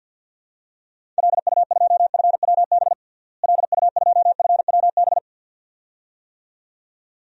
Morse code, heard as a keyed tone, about 52 wpm, 700 Hz. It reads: LU2FWB LU2FWB
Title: DXCC-100 Call Signs - 6 Characters - Repeated 2x - 50wpm